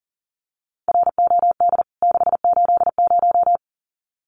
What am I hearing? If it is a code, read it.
Morse code, heard as a keyed tone, about 40 wpm, 700 Hz. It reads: ROD 680
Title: CWT Exchange 40wpm